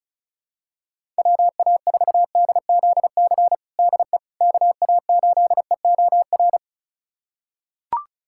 Morse code, heard as a keyed tone, about 34 wpm, 700 Hz. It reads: WA4DZC DE KA8EOR
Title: USA Callsign DE Callsign - 6 Characters - 35wpm